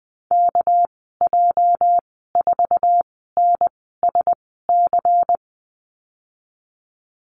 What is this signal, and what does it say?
Morse code, heard as a keyed tone, about 20 wpm, spaced as for 15 wpm, 700 Hz. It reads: KJ4NSC
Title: 207 Lesson 33g - Letter j - Call Signs 20wpm